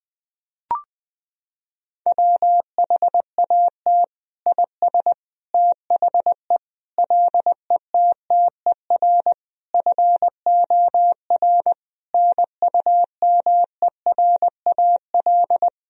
Morse code, heard as a keyed tone, about 21 wpm, 700 Hz. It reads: WHAT IS THE LETTER FOR NUMERAL
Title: Sentences from Top 1000 Words - Review Part 2 - 20wpm